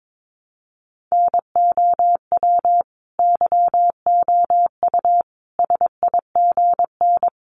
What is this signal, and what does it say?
Morse code, heard as a keyed tone, about 22 wpm, 700 Hz. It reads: NOW YOU SIGN
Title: Sentences from Top 1000 Words 22wpm